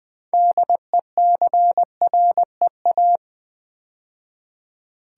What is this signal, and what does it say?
Morse code, heard as a keyed tone, about 20 wpm, 700 Hz. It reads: DECREA
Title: Build A Word - 2 Letters At A Time - 8-Letter Words - 20wpm